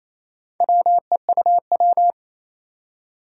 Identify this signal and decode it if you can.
Morse code, heard as a keyed tone, about 27 wpm, 700 Hz. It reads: WEUW
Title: Four Challenging Letter-Number Combinations 28wpm